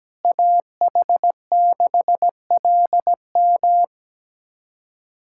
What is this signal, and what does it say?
Morse code, heard as a keyed tone, about 17 wpm, 700 Hz. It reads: AH6LM
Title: USA Callsign DE Callsign - 5 Characters - 17wpm